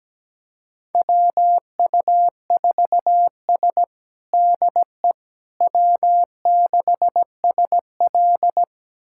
Morse code, heard as a keyed tone, about 17 wpm, 700 Hz. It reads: WU4S DE W6SL
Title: USA Callsign DE Callsign - 4 Characters - 17wpm